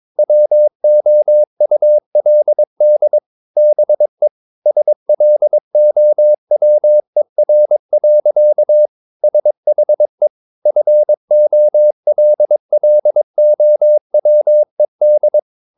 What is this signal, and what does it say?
Morse code, heard as a keyed tone, about 22 wpm, 600 Hz. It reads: WOULD BE SLOWER. SHE FOLLOWED
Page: Romanen Never af Ken Follet på engelsk.